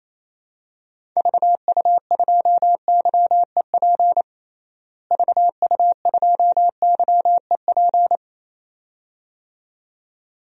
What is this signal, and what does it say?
Morse code, heard as a keyed tone, about 28 wpm, 700 Hz. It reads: VU2YEP VU2YEP
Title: DXCC-100 Call Signs - 6 Characters - Repeated 2x - 28wpm